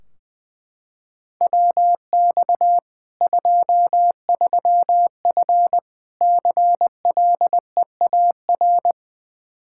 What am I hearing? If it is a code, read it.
Morse code, heard as a keyed tone, about 20 wpm, 700 Hz. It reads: WX 23F CLEAR
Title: K1USN SST QSO Practice 20wpm